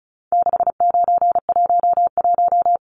Morse code, heard as a keyed tone, about 34 wpm, 700 Hz. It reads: 6911
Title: Numbers - 4-Digit 35wpm